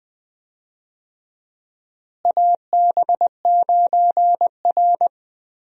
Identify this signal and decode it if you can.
Morse code, heard as a keyed tone, about 21 wpm, 700 Hz. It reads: AB9R
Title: USA Call Signs - 4 Characters - Repeated 3x - 20wpm